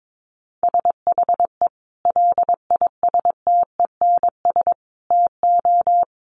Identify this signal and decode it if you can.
Morse code, heard as a keyed tone, about 22 wpm, 700 Hz. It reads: SHE LISTENS TO
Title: Easy Phrases 22wpm